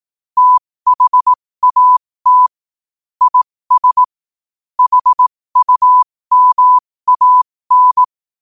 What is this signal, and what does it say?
Morse code, heard as a keyed tone, about 18 wpm, spaced as for 15 wpm, 1000 Hz. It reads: THAT IS HUMAN
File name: qotdfiles / 15wpm